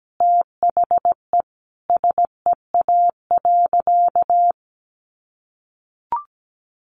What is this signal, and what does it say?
Morse code, heard as a keyed tone, about 17 wpm, 700 Hz. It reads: THE SEA.
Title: Sentences from Top 1300 Words 17wpm